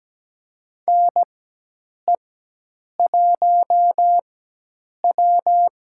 Morse code, heard as a keyed tone, about 17 wpm, spaced as for 8 wpm, 700 Hz. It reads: NE1W
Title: USA Call Signs - 4 Characters - 4x Character Spacing - 17wpm